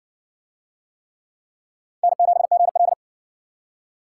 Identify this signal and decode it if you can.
Morse code, heard as a keyed tone, about 60 wpm, 700 Hz. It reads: N7GL